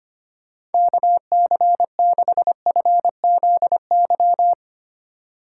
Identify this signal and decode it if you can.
Morse code, heard as a keyed tone, about 32 wpm, 700 Hz. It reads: KC6FZY